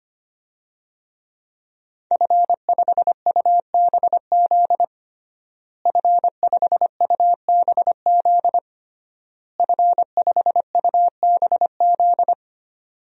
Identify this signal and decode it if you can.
Morse code, heard as a keyed tone, about 25 wpm, 700 Hz. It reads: F5UBZ F5UBZ F5UBZ
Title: DXCC-100 Call Signs - 5 Characters - Repeated 3x - 25wpm